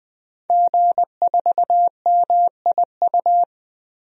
Morse code, heard as a keyed tone, about 20 wpm, 700 Hz. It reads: G4MIU
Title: DXCC-100 Call Signs - 5 Characters - Repeated 2x - 20wpm